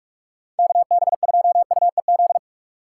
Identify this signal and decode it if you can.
Morse code, heard as a keyed tone, about 43 wpm, 700 Hz. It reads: KB2UEZ
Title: USA Call Signs - 6 Characters - 6x Character Spacing - 45wpm